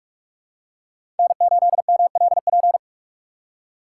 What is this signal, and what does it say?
Morse code, heard as a keyed tone, about 46 wpm, 700 Hz. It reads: N8MLP
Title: USA Call Signs - 5 Characters - 3x Character Spacing - 45wpm